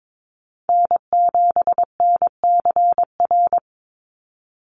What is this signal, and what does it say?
Morse code, heard as a keyed tone, about 22 wpm, 700 Hz. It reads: N7NCR
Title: USA Call Signs 22wpm